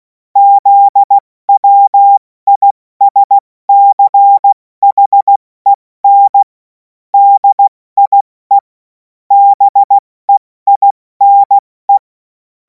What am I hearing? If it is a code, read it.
Morse code, heard as a keyed tone, about 16 wpm, 800 Hz. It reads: ZWISCHEN DIE BEINE
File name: Morsekurs / Witze / Witz 11. August 2026 70 F80